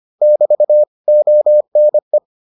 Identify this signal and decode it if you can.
Morse code, heard as a keyed tone, about 25 wpm, 600 Hz. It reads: <BT> ONE